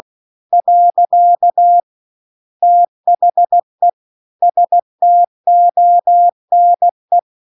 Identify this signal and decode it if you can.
Morse code, heard as a keyed tone, about 16 wpm, 700 Hz. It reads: . THE STONE
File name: CW-The-Big-Twoheaded-River-1-16WPM